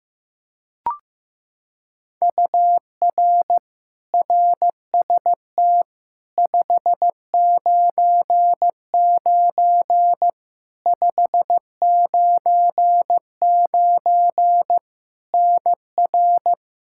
Morse code, heard as a keyed tone, about 15 wpm, 700 Hz. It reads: UR RST 599 599 NR
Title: K1USN SST QSO Practice 15wpm